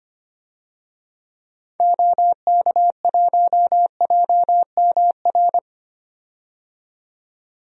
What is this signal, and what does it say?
Morse code, heard as a keyed tone, about 25 wpm, 700 Hz. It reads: OK1JMR
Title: DXCC-100 Call Signs - 6 Characters - 25wpm